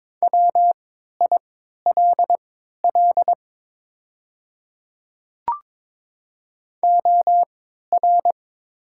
Morse code, heard as a keyed tone, about 21 wpm, spaced as for 12 wpm, 700 Hz. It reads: WILL OR
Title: 3x Character Spacing - Top 100 Words - 22wpm